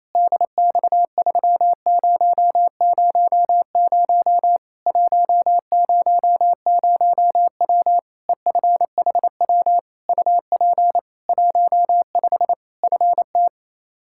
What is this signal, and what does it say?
Morse code, heard as a keyed tone, about 28 wpm, 700 Hz. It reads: DX3000 100W EFHW UP 15 FT
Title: Practice QSO - Exchange 2 - Info - 28wpm